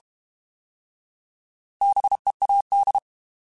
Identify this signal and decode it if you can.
Morse code, heard as a keyed tone, about 32 wpm, 800 Hz. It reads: BEAD
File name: Alphabetical / Alphabetical mp3s / Bs @30